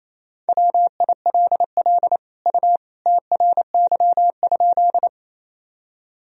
Morse code, heard as a keyed tone, about 28 wpm, 700 Hz. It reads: WILL U TRY?
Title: Sentences from Top 200 Words and Abbreviations 28wpm